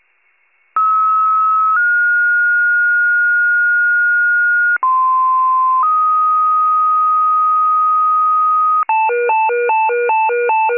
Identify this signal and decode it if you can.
Two-tone page, heard as a one-second tone, then a three-second tone, about 1290 then 1430 Hz, about 1040 then 1250 Hz, with noise. It is heard from 0.8 to 8.8 s.
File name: wcalert